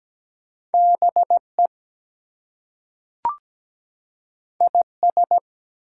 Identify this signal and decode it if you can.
Morse code, heard as a keyed tone, about 17 wpm, 700 Hz. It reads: BE IS
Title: Two Letter Words - 17wpm